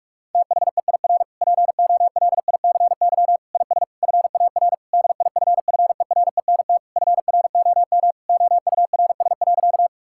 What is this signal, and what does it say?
Morse code, heard as a keyed tone, about 46 wpm, 690 Hz. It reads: THEIR POLICY IS FAR DIFFERENT FROM OURS.
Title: Sentences from Top 1000 Words - Review Part 5 - 45wpm